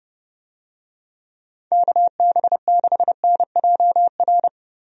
Morse code, heard as a keyed tone, about 32 wpm, 700 Hz. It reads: KB6NJR
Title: USA Call Signs - 6 Characters - Repeated 3x - 30wpm